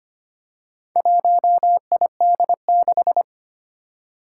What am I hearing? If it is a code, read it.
Morse code, heard as a keyed tone, about 25 wpm, 700 Hz. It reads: 1ID6
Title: Four Challenging Letter-Number Combinations 25wpm